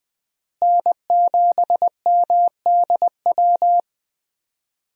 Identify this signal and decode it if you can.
Morse code, heard as a keyed tone, about 20 wpm, 700 Hz. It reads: N7MDW
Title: USA Call Signs - 5 Characters - Repeated 2x - 20wpm